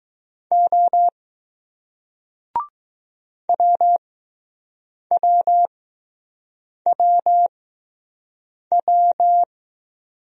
Morse code, heard as a keyed tone, about 20 wpm, spaced as for 6 wpm, 700 Hz. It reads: O WWWW